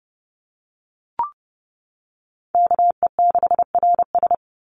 Morse code, heard as a keyed tone, about 30 wpm, 700 Hz. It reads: KE6RS
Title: NA SOTA Chaser Callsigns 30wpm